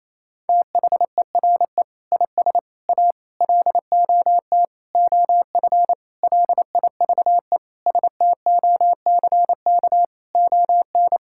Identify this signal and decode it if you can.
Morse code, heard as a keyed tone, about 28 wpm, 700 Hz. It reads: THERE IS A LOT OF LIVE STOCK ON
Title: Sentences from Top 1000 Words - Review Part 5 - 28wpm